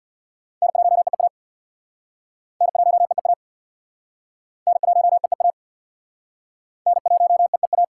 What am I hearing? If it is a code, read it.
Morse code, heard as a keyed tone, about 65 wpm, 700 Hz. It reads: N1EEA N1EEA N1EEA N1EEA